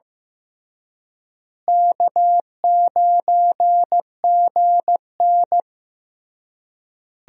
Morse code, heard as a keyed tone, about 15 wpm, 700 Hz. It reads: K9GN
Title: USA Call Signs - 4 Characters - Repeated 3x - 15wpm